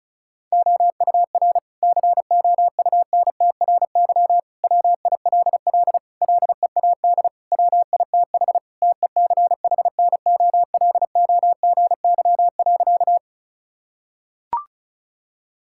Morse code, heard as a keyed tone, about 34 wpm, 700 Hz. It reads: OUR COUNTRY WILL LEAD WITH TECHNOLOGY.
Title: Sentences from Top 1000 Words - Review Part 5 - 35wpm